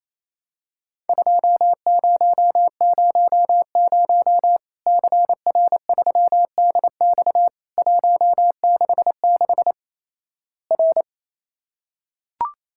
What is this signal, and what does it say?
Morse code, heard as a keyed tone, about 28 wpm, 700 Hz. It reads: 2000 CR3DX 166 R
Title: Worked All Europe DX Contest - QTC Exchange - 28wpm